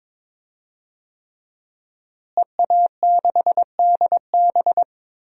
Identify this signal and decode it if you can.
Morse code, heard as a keyed tone, about 22 wpm, 700 Hz. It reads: EA6DB
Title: DXCC-100 Call Signs - 5 Characters - Repeated 3x - 22wpm